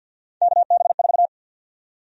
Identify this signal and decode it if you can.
Morse code, heard as a keyed tone, about 53 wpm, 700 Hz. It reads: KB4
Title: USA Call Sign Prefixes - 50wpm